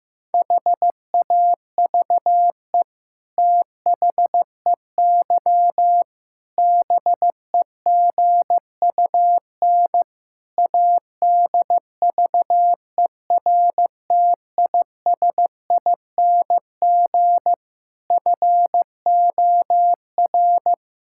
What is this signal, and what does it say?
Morse code, heard as a keyed tone, about 15 wpm, 700 Hz. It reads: HAVE THEY BEGUN ADVERTISING FOR
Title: Sentences from Top 1000 Words - Review Part 6 - 15wpm